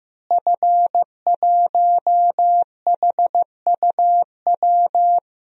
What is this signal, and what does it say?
Morse code, heard as a keyed tone, about 15 wpm, 700 Hz. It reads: F1HUW
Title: DXCC-100 Call Signs - 5 Characters - Repeated 3x - 15wpm